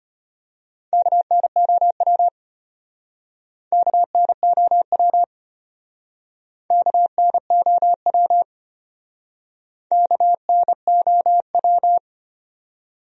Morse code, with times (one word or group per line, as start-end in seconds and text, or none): 0.9-2.3 KNOW
3.7-5.2 KNOW
6.7-8.4 KNOW
9.9-12.0 KNOW